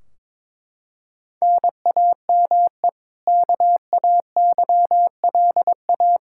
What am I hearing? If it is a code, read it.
Morse code, heard as a keyed tone, about 22 wpm, 700 Hz. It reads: NAME KAYLA
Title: Practice QSO - Exchange 1 - Beginning - 22wpm